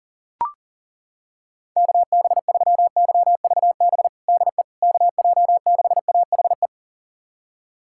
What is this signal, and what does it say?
Morse code, heard as a keyed tone, about 40 wpm, 700 Hz. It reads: KB3YVB DE KJ6AHE
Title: USA Callsign DE Callsign - 6 Characters - 40wpm